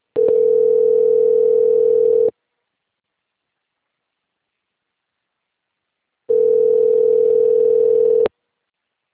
Ringback tone (North America), 440 + 480 Hz.